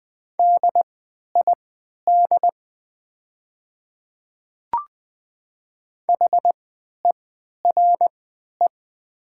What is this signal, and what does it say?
Morse code, heard as a keyed tone, about 20 wpm, spaced as for 11 wpm, 700 Hz. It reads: DID HERE
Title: 3x Character Spacing - Top 300 Words - 20wpm